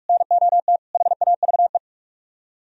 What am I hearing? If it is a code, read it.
Morse code, heard as a keyed tone, about 45 wpm, 700 Hz. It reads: NOT HAVE
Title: Speed-Racing - Sets of 4 Words 45wpm